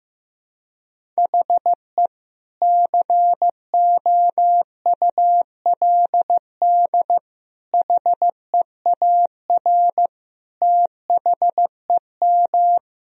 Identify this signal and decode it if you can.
Morse code, heard as a keyed tone, about 16 wpm, 700 Hz. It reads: HE COULD HEAR THEM
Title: Sentences from Top 1000 Words - Review Part 1 - 15wpm